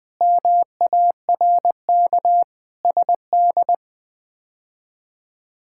Morse code, heard as a keyed tone, about 20 wpm, 700 Hz. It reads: MARK SD
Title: K1USN SST QSO Practice - 20wpm